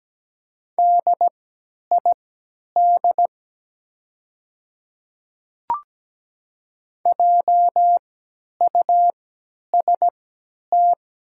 Morse code, heard as a keyed tone, about 17 wpm, spaced as for 10 wpm, 700 Hz. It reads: DID JUST
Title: 3x Character Spacing - Top 200 Words - 17wpm